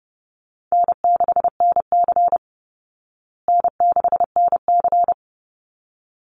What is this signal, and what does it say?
Morse code, heard as a keyed tone, about 30 wpm, 700 Hz. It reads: N6NC N6NC
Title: Call Signs Repeated 3x - 30wpm